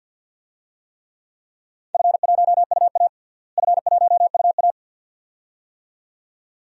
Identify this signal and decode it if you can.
Morse code, heard as a keyed tone, about 52 wpm, 700 Hz. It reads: F1UA F1UA
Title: DXCC-100 Call Signs - 4 Characters - Repeated 2x - 50wpm